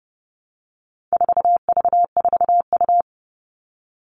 Morse code, heard as a keyed tone, about 30 wpm, 700 Hz. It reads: 4V4U
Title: All Cut Numbers - 4-Digit 30wpm